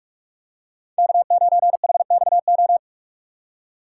Morse code, heard as a keyed tone, about 44 wpm, 700 Hz. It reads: K9HXO
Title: USA Call Signs - 5 Characters - 6x Character Spacing - 45wpm